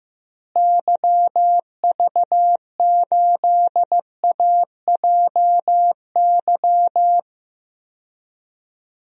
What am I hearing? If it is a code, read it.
Morse code, heard as a keyed tone, about 15 wpm, 700 Hz. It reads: YV8AJY
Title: DXCC-100 Call Signs - 6 Characters - Repeated 3x - 15wpm